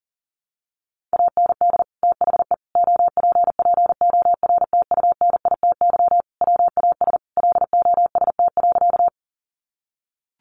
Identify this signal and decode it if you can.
Morse code, heard as a keyed tone, about 40 wpm, 700 Hz. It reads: AND THE OPPORTUNITY WAS LOST.
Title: Sentences from Top 1400 Words 40wpm